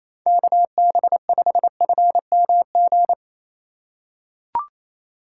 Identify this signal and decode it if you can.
Morse code, heard as a keyed tone, about 28 wpm, 700 Hz. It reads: KB5FMG
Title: USA Call Signs - 6 Characters - 2x Character Spacing - 28wpm